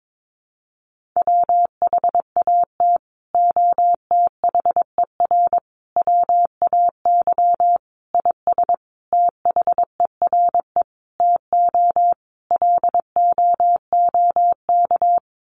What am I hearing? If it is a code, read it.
Morse code, heard as a keyed tone, about 22 wpm, 700 Hz. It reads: WHAT OTHER WAY IS THERE TO LOOK
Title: Sentences from Top 100 Words 22wpm